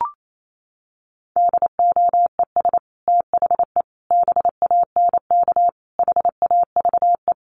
Morse code, heard as a keyed tone, about 28 wpm, 700 Hz. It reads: DOES THE BANK HAVE
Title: Sentences from Top 1500 Words 28wpm